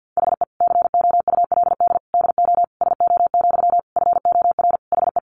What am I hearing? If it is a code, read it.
Morse code, heard as a keyed tone, about 50 wpm, 700 Hz. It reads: HE COULD DO SO, FOR HE
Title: Sentences from Top 1500 Words 50wpm